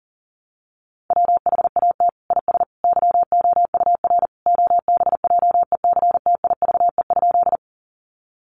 Morse code, heard as a keyed tone, about 40 wpm, 700 Hz. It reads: WHAT IS YOUR OBJECTIVE?
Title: Sentences from Top 1200 Words 40wpm